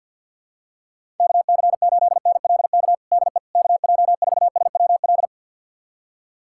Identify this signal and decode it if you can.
Morse code, heard as a keyed tone, about 52 wpm, 700 Hz. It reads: KC8NLK DE KJ4SWL
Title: USA Callsign DE Callsign 50wpm